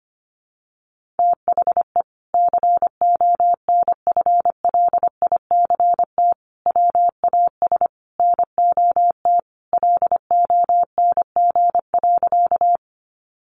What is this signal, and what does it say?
Morse code, heard as a keyed tone, about 25 wpm, 700 Hz. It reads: THE CONFLICT WAS NOT LONG.
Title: Sentences from Top 1300 Words 25wpm